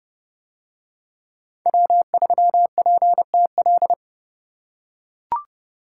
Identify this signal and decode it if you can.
Morse code, heard as a keyed tone, about 30 wpm, 700 Hz. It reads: W3PTL